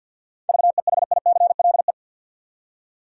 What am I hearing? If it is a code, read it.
Morse code, heard as a keyed tone, about 49 wpm, 700 Hz. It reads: VEHICLE
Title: Seven Letter Words - 50wpm